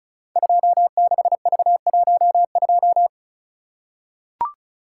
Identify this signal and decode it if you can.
Morse code, heard as a keyed tone, about 36 wpm, 700 Hz. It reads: 26V12
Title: Code Groups - 5 Letter:Numbers - 35wpm